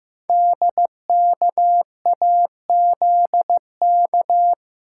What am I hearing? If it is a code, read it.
Morse code, heard as a keyed tone, about 15 wpm, 700 Hz. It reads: DKAZK
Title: Code Groups - 5 Letters - 15wpm